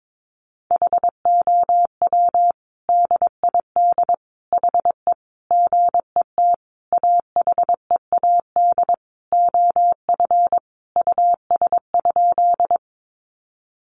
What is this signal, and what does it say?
Morse code, heard as a keyed tone, about 22 wpm, 700 Hz. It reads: HOW DID HE GET AHEAD OF US?
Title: Sentences from Top 1100 Words 22wpm